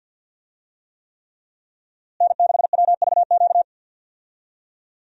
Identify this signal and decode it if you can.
Morse code, heard as a keyed tone, about 47 wpm, 700 Hz. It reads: N6WVQ
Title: USA Call Signs - 5 Characters - Repeated 3x - 50wpm